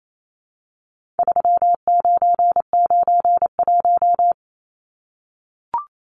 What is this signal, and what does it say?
Morse code, heard as a keyed tone, about 28 wpm, 700 Hz. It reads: 3991
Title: Numbers - 4-Digit 28wpm